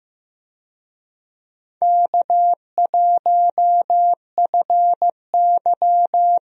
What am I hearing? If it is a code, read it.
Morse code, heard as a keyed tone, about 15 wpm, 700 Hz. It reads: K1FY
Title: USA Call Signs - 4 Characters - Repeated 2x - 15wpm